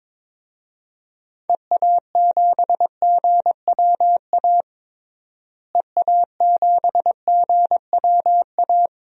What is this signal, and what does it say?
Morse code, heard as a keyed tone, about 21 wpm, 700 Hz. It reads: EA7GWA EA7GWA
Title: DXCC-100 Call Signs - 6 Characters - Repeated 3x - 22wpm